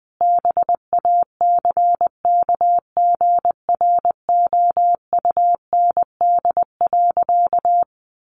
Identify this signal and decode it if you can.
Morse code, heard as a keyed tone, about 20 wpm, 700 Hz. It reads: BACKGROUND.
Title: Sentences from Top 1300 Words 20wpm